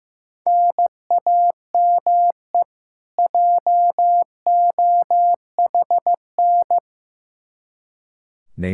Morse code, heard as a keyed tone, about 15 wpm, 700 Hz. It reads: NAME JOHN
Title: Practice QSO - Exchange 1 - Beginning - 15wpm